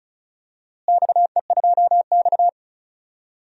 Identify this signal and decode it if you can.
Morse code, heard as a keyed tone, about 36 wpm, 700 Hz. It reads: XE2X
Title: DXCC-100 Call Signs - 4 Characters - Repeated 2x - 35wpm